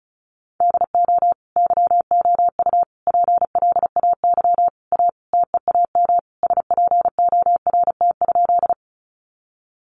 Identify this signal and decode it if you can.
Morse code, heard as a keyed tone, about 34 wpm, 700 Hz. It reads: DO YOU PLAY A TEAM SPORT?
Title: Sentences from Top 1500 Words 35wpm